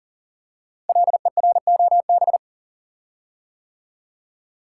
Morse code, heard as a keyed tone, about 40 wpm, 690 Hz. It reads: LEROB
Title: Code Groups - 5 Letters - 40wpm